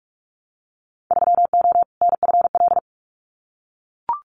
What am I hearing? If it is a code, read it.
Morse code, heard as a keyed tone, about 44 wpm, 700 Hz. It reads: 3O NFL
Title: Winter Field Day Exchanges 45wpm